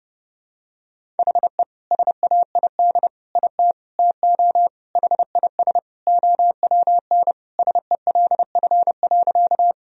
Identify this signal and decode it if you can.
Morse code, heard as a keyed tone, about 32 wpm, 700 Hz. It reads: HE SAID IT TO HIS OWN SELF.
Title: Sentences from Top 1000 Words - Review Part 1 - 30wpm